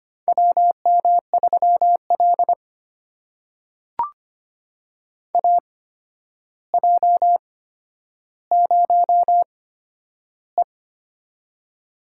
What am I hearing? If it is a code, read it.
Morse code, heard as a keyed tone, about 25 wpm, spaced as for 11 wpm, 700 Hz. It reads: WM3L AJ0E
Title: USA Call Signs - 4 Characters - 8x Character Spacing - 25wpm